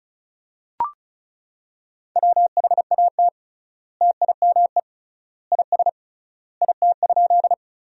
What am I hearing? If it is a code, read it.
Morse code, heard as a keyed tone, about 34 wpm, 700 Hz. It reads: WHAT TIME IS IT?
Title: 3x Word-Spacing - Sentences from Top 100 Words - 35wpm